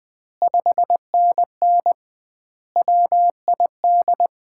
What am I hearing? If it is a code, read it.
Morse code, heard as a keyed tone, about 21 wpm, 700 Hz. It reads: 5NN WID
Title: Common QSO Phrases - 2x Word-Spacing - 20wpm